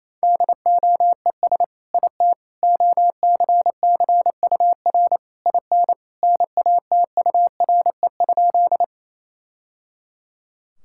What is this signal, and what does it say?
Morse code, heard as a keyed tone, about 28 wpm, 700 Hz. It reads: DOES IT OCCUR IN NATURE?
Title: Sentences from Top 1000 Words - Review Part 6 - 28wpm